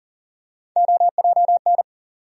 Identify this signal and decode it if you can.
Morse code, heard as a keyed tone, about 40 wpm, 700 Hz. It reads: OJN
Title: USA Call Sign Postfixes - 3 Letters 40wpm